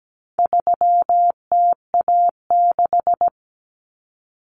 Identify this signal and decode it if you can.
Morse code, heard as a keyed tone, about 17 wpm, 700 Hz. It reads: 3TA6
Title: Common Cut Numbers - 4-Digit 17wpm